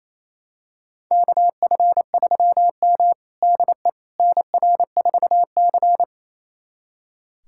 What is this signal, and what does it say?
Morse code, heard as a keyed tone, about 28 wpm, 700 Hz. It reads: KF3M DE NR4C
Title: USA Callsign DE Callsign - 4 Characters - 28wpm